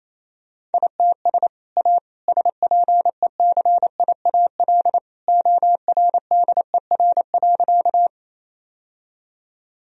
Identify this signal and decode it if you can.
Morse code, heard as a keyed tone, about 28 wpm, 700 Hz. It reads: ITS A SPECIAL ORDER.
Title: Sentences from Top 1000 Words - Review Part 3 - 28wpm